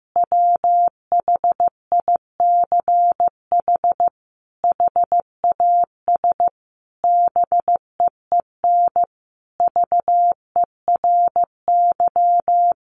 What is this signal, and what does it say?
Morse code, heard as a keyed tone, about 14 wpm, 700 Hz. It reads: WHICH HAS BEEN VERY
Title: Sentences from Top 1400 Words 15wpm